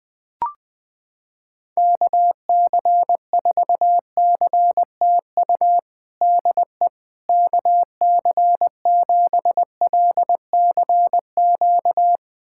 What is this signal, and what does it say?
Morse code, heard as a keyed tone, about 21 wpm, 700 Hz. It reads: KC4CTU DE KC7LCQ
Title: USA Callsign DE Callsign - 6 Characters - 20wpm